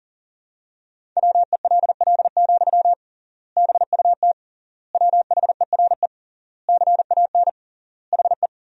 Morse code, heard as a keyed tone, about 37 wpm, 700 Hz. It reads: WELL, BUT WHERE CAN HE
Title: 3x Word-Spacing - Sentences from Top 500 Words - 40wpm